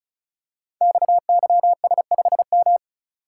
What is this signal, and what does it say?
Morse code, heard as a keyed tone, about 36 wpm, 700 Hz. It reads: XYS5M
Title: Code Groups - 5 Letter:Numbers - 35wpm